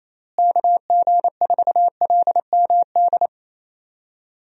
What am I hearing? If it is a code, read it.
Morse code, heard as a keyed tone, about 28 wpm, 700 Hz. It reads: KG4LMD
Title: USA Call Signs - 6 Characters - Repeated 2x - 28wpm